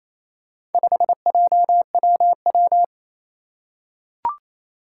Morse code, heard as a keyed tone, about 28 wpm, 700 Hz. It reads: 5JWW
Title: Four Challenging Letter-Number Combinations 28wpm